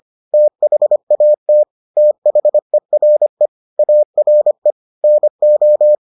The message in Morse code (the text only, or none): THAT THERE ARE NO